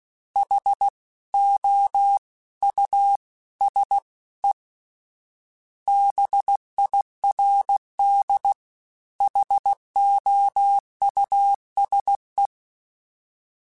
Morse code, heard as a keyed tone, about 16 wpm, 800 Hz. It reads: H O U S E BIRD HOUSE
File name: General / General-mp3s / Houses @15